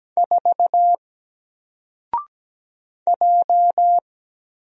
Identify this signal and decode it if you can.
Morse code, heard as a keyed tone, about 17 wpm, 700 Hz. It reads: 4 J
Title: Single Character (Letter:Number:Punctuation:Prosign) 17wpm